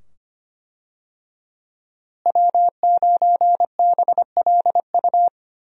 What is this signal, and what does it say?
Morse code, heard as a keyed tone, about 25 wpm, 700 Hz. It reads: W9BLU
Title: USA Call Signs - 5 Characters - Repeated 2x - 25wpm